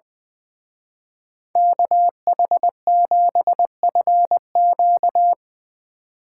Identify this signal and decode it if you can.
Morse code, heard as a keyed tone, about 22 wpm, 700 Hz. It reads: KH7FQ
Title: Speed Racing - USA 2x2 Call Signs 20wpm